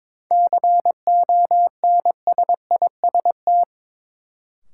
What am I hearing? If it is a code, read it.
Morse code, heard as a keyed tone, about 22 wpm, 700 Hz. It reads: CONSIST
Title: Sentences from Top 1400 Words - New Word Drill - 22wpm